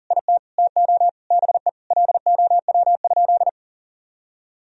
Morse code, heard as a keyed tone, about 40 wpm, 700 Hz. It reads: IT TO BE LOW?
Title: Sentences from Top 1000 Words - Review Part 1 - 40wpm